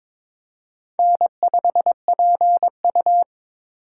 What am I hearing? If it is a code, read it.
Morse code, heard as a keyed tone, about 22 wpm, 700 Hz. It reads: N5PU
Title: K1USN SST QSO Practice - 22wpm